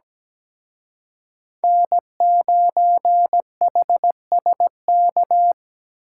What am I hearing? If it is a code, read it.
Morse code, heard as a keyed tone, about 20 wpm, 700 Hz. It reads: N9HSK
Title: Speed Racing - USA 1x3 Call Signs 17wpm